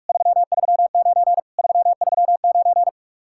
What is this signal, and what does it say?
Morse code, heard as a keyed tone, about 51 wpm, 700 Hz. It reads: 339 339
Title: Speed-Racing Common QSO Phrases Encore - 45wpm